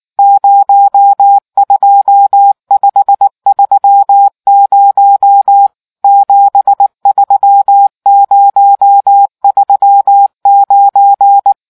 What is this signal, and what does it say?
Morse code, heard as a keyed tone, about 19 wpm, 800 Hz. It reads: 02530 73039